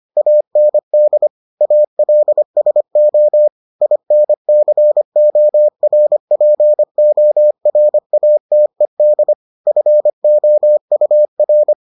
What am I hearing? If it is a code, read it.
Morse code, heard as a keyed tone, about 25 wpm, 600 Hz. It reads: AND ALSO INCORPORATED FOUR